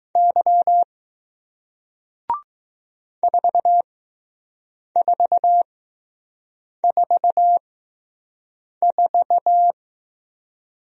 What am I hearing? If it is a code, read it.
Morse code, heard as a keyed tone, about 20 wpm, spaced as for 6 wpm, 700 Hz. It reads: Y 4444